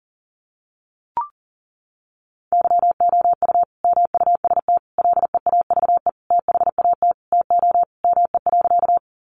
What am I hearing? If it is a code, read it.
Morse code, heard as a keyed tone, about 40 wpm, 700 Hz. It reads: YOU MUST LEAVE THAT TO ME.
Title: Sentences from Top 400 Words 40wpm